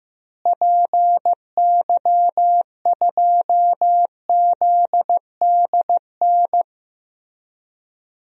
Morse code, heard as a keyed tone, about 15 wpm, 700 Hz. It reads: PY2ZDN
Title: DXCC-100 Call Signs - 6 Characters - Repeated 3x - 15wpm